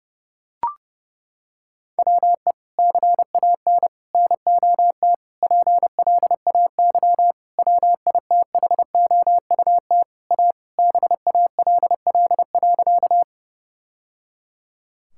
Morse code, heard as a keyed tone, about 32 wpm, 700 Hz. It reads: WE CAN NOT PLAY WITHOUT A BALL.
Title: Sentences from Top 1000 Words - Review Part 6 - 30wpm